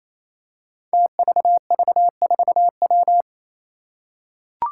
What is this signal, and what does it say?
Morse code, heard as a keyed tone, about 28 wpm, 700 Hz. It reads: TVV4W
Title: Five Challenging Letter-Number Combinations 28wpm